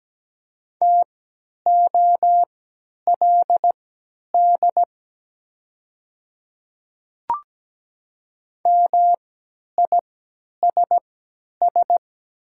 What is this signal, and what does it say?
Morse code, heard as a keyed tone, about 17 wpm, spaced as for 10 wpm, 700 Hz. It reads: TOLD MISS
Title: 3x Character Spacing - Top 500 Words - 17wpm